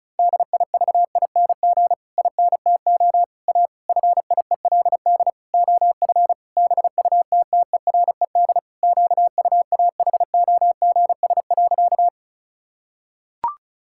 Morse code, heard as a keyed tone, about 36 wpm, 700 Hz. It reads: DIVING INTO A FIELD OF BUTTERED QUAHOGS.
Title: Pangram Challenge v2 - 35wpm